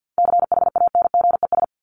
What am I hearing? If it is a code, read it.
Morse code, heard as a keyed tone, about 50 wpm, 700 Hz. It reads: CHANGES
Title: ES:S Suffix - Top 100 Words with Base Words 50wpm